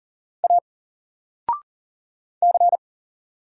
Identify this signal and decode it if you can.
Morse code, heard as a keyed tone, about 40 wpm, 700 Hz. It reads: A C
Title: Single Letter-Number 40wpm